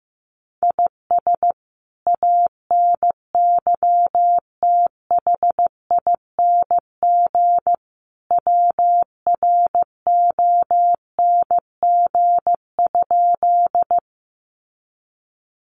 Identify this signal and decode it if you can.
Morse code, heard as a keyed tone, about 15 wpm, 700 Hz. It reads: IS ANYTHING WRONG?
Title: Sentences from Top 1000 Words 15wpm